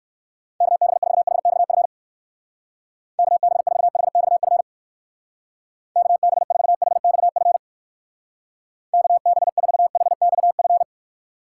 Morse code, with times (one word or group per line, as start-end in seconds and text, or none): 0.6-1.9 KB4HXF
3.2-4.6 KB4HXF
6.0-7.6 KB4HXF
8.9-10.8 KB4HXF